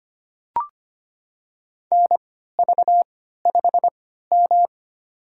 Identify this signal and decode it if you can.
Morse code, heard as a keyed tone, about 25 wpm, spaced as for 14 wpm, 700 Hz. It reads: NV5M
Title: USA Call Signs - 4 Characters - 3x Character Spacing - 25wpm